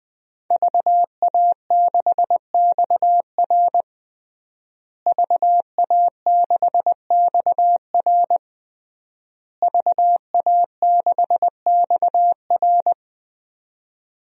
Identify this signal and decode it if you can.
Morse code, heard as a keyed tone, about 20 wpm, 700 Hz. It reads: VA6XR VA6XR VA6XR
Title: DXCC-100 Call Signs - 5 Characters - Repeated 3x - 20wpm